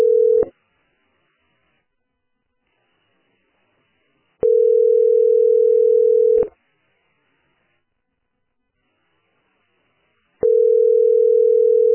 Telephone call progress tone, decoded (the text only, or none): ringback tone